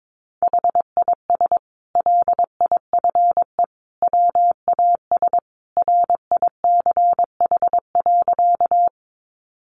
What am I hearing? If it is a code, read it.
Morse code, heard as a keyed tone, about 22 wpm, 700 Hz. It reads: HIS LIFE WAS RICH.
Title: Sentences from Top 1100 Words 22wpm